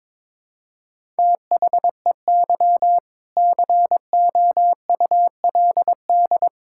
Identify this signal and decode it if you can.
Morse code, heard as a keyed tone, about 22 wpm, 700 Hz. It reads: THEY COULD
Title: Sentences from Top 1000 Words - Review Part 6 - 22wpm